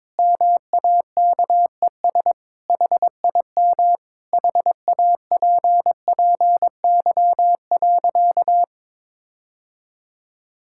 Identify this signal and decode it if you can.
Morse code, heard as a keyed tone, about 22 wpm, 700 Hz. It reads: MAKES HIM HAPPY.
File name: practice / 22wpm / Sentences with 2-Word Phrases - 22wpm